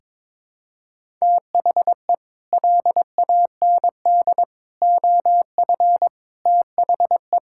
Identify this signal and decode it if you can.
Morse code, heard as a keyed tone, about 22 wpm, 700 Hz. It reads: THE LAND OF THE
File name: practice / 22wpm / Sets of 8 Words - 22wpm